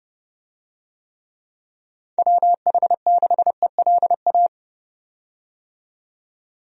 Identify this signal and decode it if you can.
Morse code, heard as a keyed tone, about 29 wpm, 700 Hz. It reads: WH6ELA
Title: USA Call Signs - 6 Characters - 30wpm